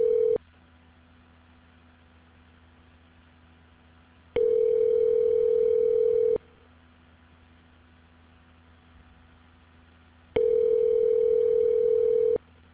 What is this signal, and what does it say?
Ringback tone (North America), 440 + 480 Hz.